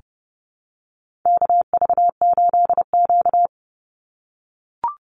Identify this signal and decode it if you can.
Morse code, heard as a keyed tone, about 30 wpm, 700 Hz. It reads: KV8Q
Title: Call Signs - 30wpm